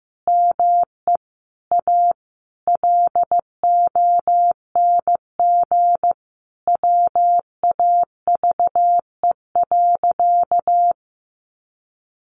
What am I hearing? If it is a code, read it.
Morse code, heard as a keyed tone, about 15 wpm, 700 Hz. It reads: ME A LONG WAVE.
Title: Sentences from Top 1400 Words 15wpm